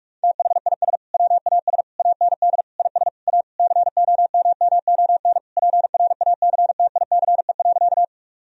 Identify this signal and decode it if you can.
Morse code, heard as a keyed tone, about 41 wpm, 700 Hz. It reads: THIS WAS AND IS A COMMON PRACTICE.
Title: Sentences from Top 1000 Words - Review Part 4 - 45wpm